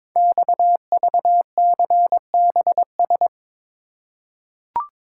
Morse code, heard as a keyed tone, about 22 wpm, 700 Hz. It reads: XVCBS
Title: Code Groups - 5 Letters - 22wpm